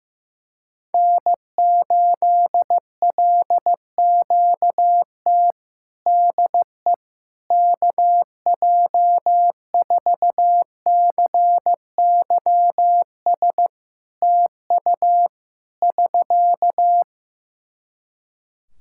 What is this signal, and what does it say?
Morse code, heard as a keyed tone, about 14 wpm, 700 Hz. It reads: N8LQT DE KJ4CYS TU <SK>
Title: K1USN SST QSO Practice 15wpm